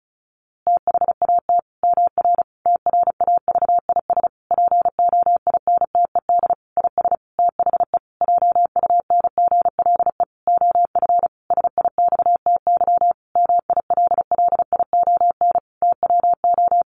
Morse code, heard as a keyed tone, about 34 wpm, 700 Hz. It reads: THAT MR TRAVIS POINTED IS THE JUNGLE OF SIXTY MILLION TWO
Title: A Sound Of Thunder 35wpm